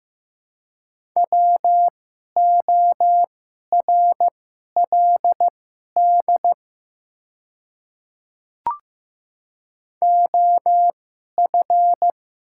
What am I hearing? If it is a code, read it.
Morse code, heard as a keyed tone, about 15 wpm, spaced as for 11 wpm, 700 Hz. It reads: WORLD OF
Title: 2x Character Spacing - Top 200 Words - 15wpm